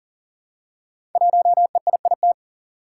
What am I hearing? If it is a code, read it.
Morse code, heard as a keyed tone, about 41 wpm, 700 Hz. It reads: 1EIIT
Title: Five Challenging Letter-Number Combinations 40wpm